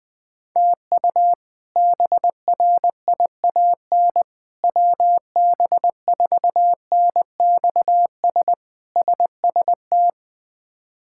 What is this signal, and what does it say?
Morse code, heard as a keyed tone, about 20 wpm, 700 Hz. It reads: TU BRIAN WB4NXS SST
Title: K1USN SST QSO Practice - 20wpm